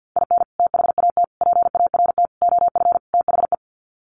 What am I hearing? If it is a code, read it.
Morse code, heard as a keyed tone, about 50 wpm, 700 Hz. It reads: IN THAT PART OF THE
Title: Sets of 5 Words Encore2 50wpm